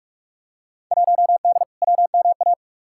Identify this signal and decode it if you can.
Morse code, heard as a keyed tone, about 42 wpm, 700 Hz. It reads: 1D WMA
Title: Field Day Exchanges - 45wpm